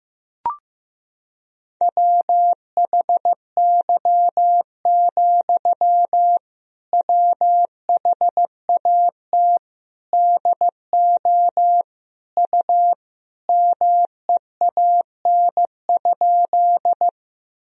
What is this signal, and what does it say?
Morse code, heard as a keyed tone, about 15 wpm, 700 Hz. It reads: WHY, WHAT DO U MEAN?
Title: Sentences from Top 200 Words and Abbreviations 15wpm